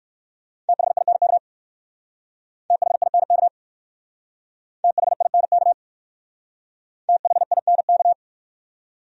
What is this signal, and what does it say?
Morse code, heard as a keyed tone, about 54 wpm, 700 Hz. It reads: THINK THINK THINK THINK
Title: Speed-Racing Top 200 Words 45wpm